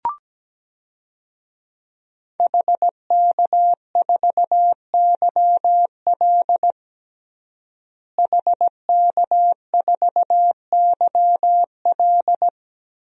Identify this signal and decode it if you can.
Morse code, heard as a keyed tone, about 17 wpm, 700 Hz. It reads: HK4YL HK4YL
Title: DXCC-100 Call Signs - 5 Characters - Repeated 3x - 17wpm